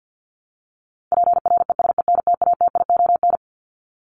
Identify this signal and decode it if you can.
Morse code, heard as a keyed tone, about 50 wpm, 700 Hz. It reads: PRESENTATION
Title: Top 1501-1750 Words 50wpm